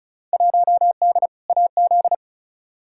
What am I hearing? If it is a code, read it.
Morse code, heard as a keyed tone, about 36 wpm, 700 Hz. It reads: 1D AZ
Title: Field Day Exchanges - 35wpm